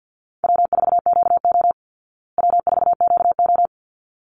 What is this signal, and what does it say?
Morse code, heard as a keyed tone, about 48 wpm, 700 Hz. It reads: W4QO W4QO
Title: Call Signs Repeated 3x - 50wpm